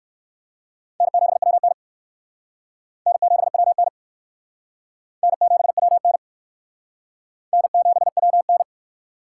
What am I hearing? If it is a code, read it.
Morse code, heard as a keyed tone, about 55 wpm, 700 Hz. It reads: N7WN N7WN N7WN N7WN
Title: Speed Racing - USA 1x2 Call Signs 45wpm